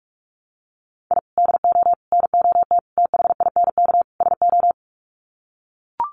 Morse code, heard as a keyed tone, about 44 wpm, 700 Hz. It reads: I DO NOT THINK SO
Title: Sets of 5 Words 45wpm